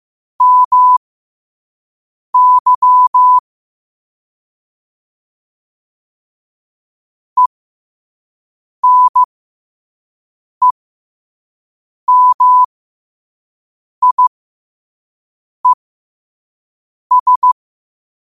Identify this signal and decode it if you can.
Morse code, heard as a keyed tone, about 15 wpm, spaced as for 5 wpm, 1000 Hz. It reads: MY ENEMIES
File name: qotdfiles / 5wpm